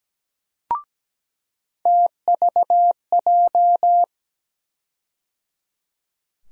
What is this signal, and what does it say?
Morse code, heard as a keyed tone, about 17 wpm, 700 Hz. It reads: TVJ
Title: USA Call Sign Postfixes - 3 Letters 17wpm